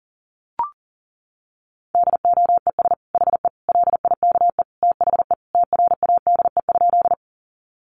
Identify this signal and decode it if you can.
Morse code, heard as a keyed tone, about 40 wpm, 700 Hz. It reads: DOES HE LIKE THE TRADE?
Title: Sentences from Top 600 Words 40wpm